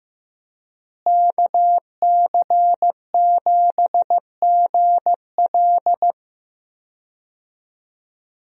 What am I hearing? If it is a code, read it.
Morse code, heard as a keyed tone, about 15 wpm, 700 Hz. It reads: KC7GL
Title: USA Call Signs - 5 Characters - Repeated 2x - 15wpm